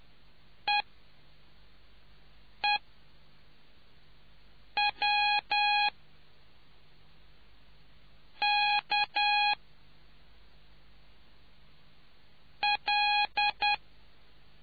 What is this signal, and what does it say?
Morse code, heard as a keyed tone, about 10 wpm, spaced as for 3 wpm, 830 Hz. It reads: EEWK L